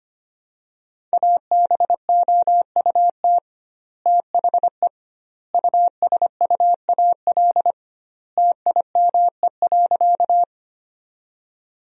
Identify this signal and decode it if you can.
Morse code, heard as a keyed tone, about 25 wpm, 700 Hz. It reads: ABOUT THE USUAL TIME.
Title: 2x Word-Spacing - Sentences from Top 400 Words - 25wpm